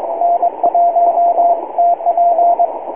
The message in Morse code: N1TP

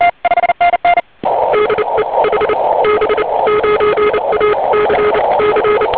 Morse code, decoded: DE HB9ACC

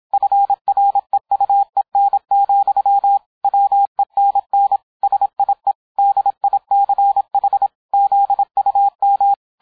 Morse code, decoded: FREUEN, WENN SIE DICH ZUM

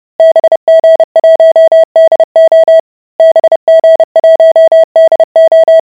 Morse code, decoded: BG1DO BG1DO

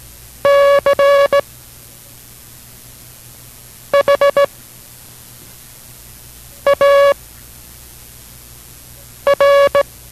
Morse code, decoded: CHAR

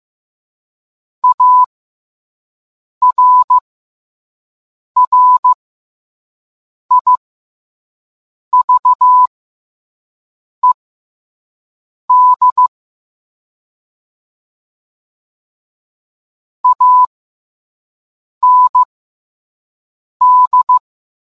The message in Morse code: ARRIVED AND

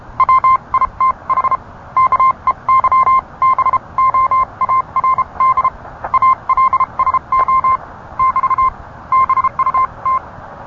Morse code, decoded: WITH KEYBOARD ALSO <BT> BUT

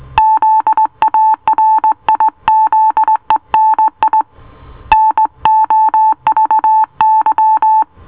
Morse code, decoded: ZARIZENI NOVY

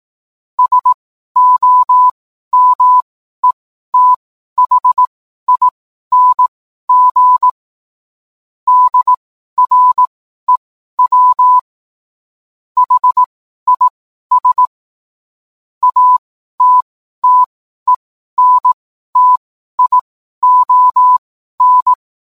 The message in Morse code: SOMETHING DREW HIS ATTENTION

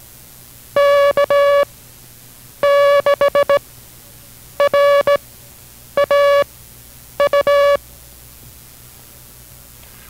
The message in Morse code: K6RAU

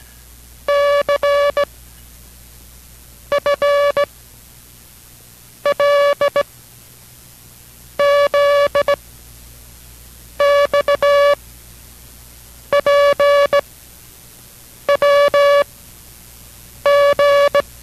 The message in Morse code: CFLZXPWG

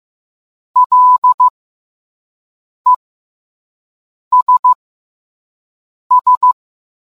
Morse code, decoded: LESS

